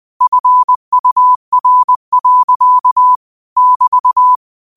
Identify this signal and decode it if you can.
Morse code, heard as a keyed tone, about 20 wpm, 1000 Hz. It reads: FUR. <BT>
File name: qotdfiles / 20wpm